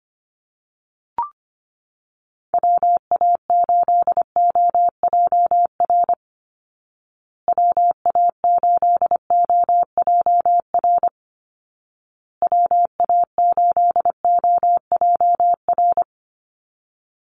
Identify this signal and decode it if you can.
Morse code, heard as a keyed tone, about 25 wpm, 700 Hz. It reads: WA8OJR WA8OJR WA8OJR
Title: Call Signs Repeated 2x - 25wpm